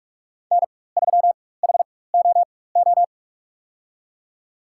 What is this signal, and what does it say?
Morse code, heard as a keyed tone, about 43 wpm, spaced as for 21 wpm, 700 Hz. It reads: N3HOO
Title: USA Call Signs - 5 Characters - 4x Character Spacing - 45wpm